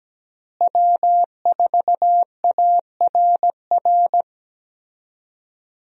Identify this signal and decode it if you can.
Morse code, heard as a keyed tone, about 17 wpm, 700 Hz. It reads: W4ARR
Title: USA Call Signs - 5 Characters - Repeated 3x - 17wpm